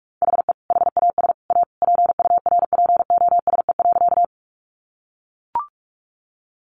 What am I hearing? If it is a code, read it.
Morse code, heard as a keyed tone, about 44 wpm, 700 Hz. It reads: HE HAS A PURPOSE.
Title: Sentences from Top 800 Words 45wpm